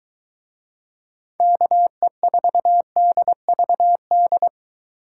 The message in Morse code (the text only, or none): KE4DVD